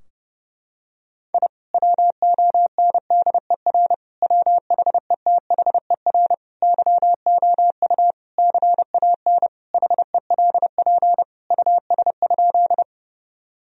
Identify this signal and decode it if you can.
Morse code, heard as a keyed tone, about 30 wpm, 700 Hz. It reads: I WONDER WHETHER YOU CAN HELP US?
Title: Sentences from Top 1000 Words - Review Part 3 - 30wpm